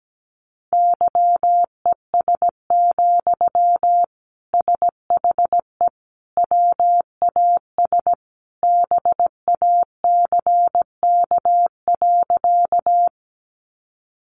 Morse code, heard as a keyed tone, about 17 wpm, 700 Hz. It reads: YES, SHE WAS BACK.
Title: Sentences from Top 500 Words 17wpm